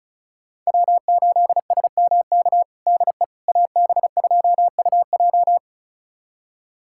Morse code, heard as a keyed tone, about 34 wpm, 700 Hz. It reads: W8SMK DE AB2UJ
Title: USA Callsign DE Callsign - 5 Characters - 35wpm